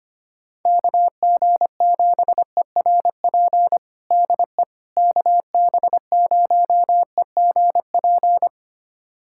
Morse code, heard as a keyed tone, about 25 wpm, 700 Hz. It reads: KG7ERP DE KB0EGP
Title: USA Callsign DE Callsign 25wpm